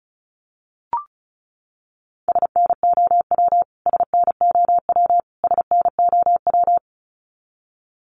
Morse code, heard as a keyed tone, about 34 wpm, 700 Hz. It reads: SNOW SNOW SNOW
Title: Top 500 Words Repeated 35wpm